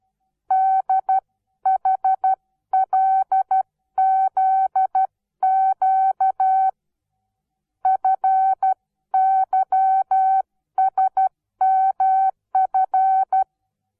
Morse code, heard as a keyed tone, about 12 wpm, 760 Hz. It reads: DHLZQ FYSMF
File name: Onlinelehrgang / morsen / DJ4UF Morsen17